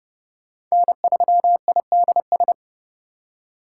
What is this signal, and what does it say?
Morse code, heard as a keyed tone, about 32 wpm, 700 Hz. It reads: N3IDS
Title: USA Call Signs - 2x Character Spacing - 30wpm